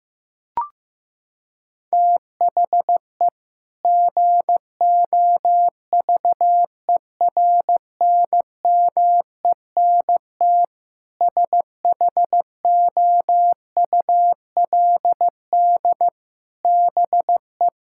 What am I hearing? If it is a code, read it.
Morse code, heard as a keyed tone, about 15 wpm, 700 Hz. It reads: THE GOVERNMENT SHOULD BE
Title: Sentences from Top 1600 Words 15wpm